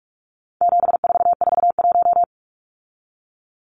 Morse code, heard as a keyed tone, about 44 wpm, 700 Hz. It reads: 7441
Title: Numbers - 4-Digit 45wpm